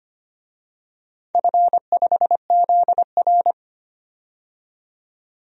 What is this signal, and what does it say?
Morse code, heard as a keyed tone, about 25 wpm, 700 Hz. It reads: F5ZR
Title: DXCC-100 Call Signs - 4 Characters - 25wpm